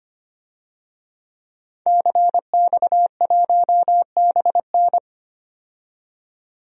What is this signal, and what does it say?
Morse code, heard as a keyed tone, about 25 wpm, 700 Hz. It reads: CX1BN
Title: DXCC-100 Call Signs - 5 Characters - 25wpm